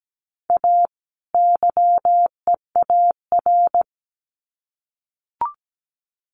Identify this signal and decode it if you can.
Morse code, heard as a keyed tone, about 17 wpm, 700 Hz. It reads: A YEAR
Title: Sets of 2 Words 17wpm